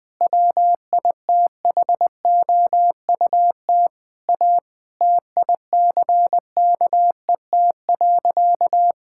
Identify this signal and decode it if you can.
Morse code, heard as a keyed tone, about 20 wpm, 700 Hz. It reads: WITHOUT A TICKET.
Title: Sentences with 6-Word Phrases - 20wpm